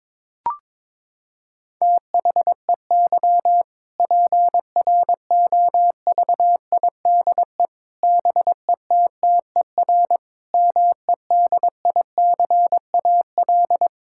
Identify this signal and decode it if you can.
Morse code, heard as a keyed tone, about 22 wpm, 700 Hz. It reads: THEY PROVIDE BETTER MEDICAL
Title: Sentences from Top 1000 Words - Review Part 5 - 22wpm